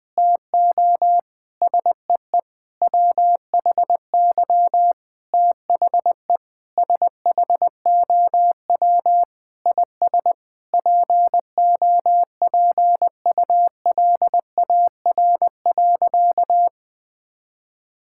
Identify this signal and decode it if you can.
Morse code, heard as a keyed tone, about 21 wpm, 700 Hz. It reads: TO SEE WHY THE SHOW IS POPULAR.
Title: Sentences with 6-Word Phrases - 20wpm